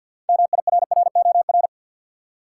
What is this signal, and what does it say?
Morse code, heard as a keyed tone, about 47 wpm, 700 Hz. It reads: MIRROR
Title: Six Letter Words - 50wpm